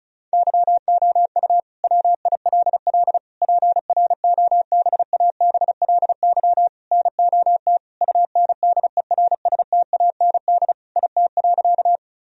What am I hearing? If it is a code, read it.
Morse code, heard as a keyed tone, about 34 wpm, 700 Hz. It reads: YOU WILL PROBABLY NOT UNDERSTAND IT.
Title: Sentences from Top 1000 Words - Review Part 4 - 35wpm